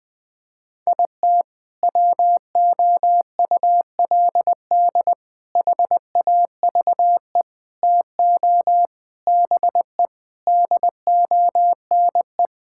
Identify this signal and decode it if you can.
Morse code, heard as a keyed tone, about 20 wpm, 700 Hz. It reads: IT WOULD HAVE TO BE DONE
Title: Sentences with 4-Word Phrases - Spoken Beginning - 20wpm